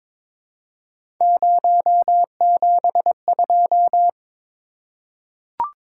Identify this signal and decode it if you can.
Morse code, heard as a keyed tone, about 22 wpm, 700 Hz. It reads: 072
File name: practice / 22wpm / Three Letter-Number Combinations - 22wpm